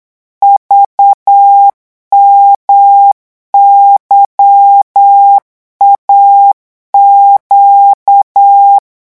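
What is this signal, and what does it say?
Morse code, heard as a keyed tone, about 8 wpm, 800 Hz. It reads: VMYAQ
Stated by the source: Morse code revision lesson at 8 words per minute, from a French amateur radio course.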